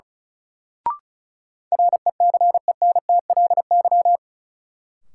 Morse code, heard as a keyed tone, about 36 wpm, 700 Hz. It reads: RECENTLY
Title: Sentences from Top 1200 Words - New Word Drill - 35wpm